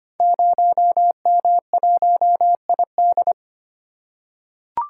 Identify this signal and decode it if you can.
Morse code, heard as a keyed tone, about 25 wpm, 700 Hz. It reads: 0M1ID
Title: Code Groups - 5 Letter:Numbers - 25wpm